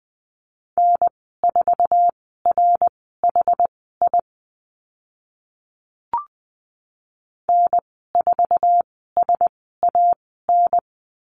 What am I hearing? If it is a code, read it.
Morse code, heard as a keyed tone, about 20 wpm, spaced as for 14 wpm, 700 Hz. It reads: N4RHI N4SAN
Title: 069 Lesson 12g - Letter d - Call Signs 20wpm